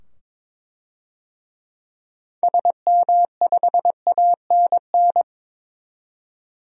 Morse code, heard as a keyed tone, about 22 wpm, 700 Hz. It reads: SM5ANN
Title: DXCC-100 Call Signs - 6 Characters - 22wpm